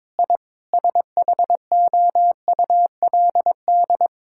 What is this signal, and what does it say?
Morse code, heard as a keyed tone, about 22 wpm, 700 Hz. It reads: I SHOULD
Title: Sentences from Top 1000 Words - Review Part 6 - 22wpm